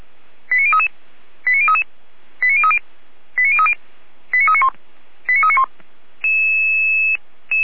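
Five-tone selective call (ZVEI), heard as quick runs of five tones, tones of about 70 ms. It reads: 89033 89033 89030 89030 89391 89391
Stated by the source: ZVEI five-tone selective call received by SDR.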